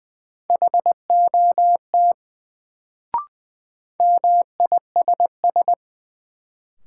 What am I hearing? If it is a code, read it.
Morse code, heard as a keyed tone, about 20 wpm, 700 Hz. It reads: HOT MISS
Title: Sentences from Top 500 Words - New Word Drill - 20wpm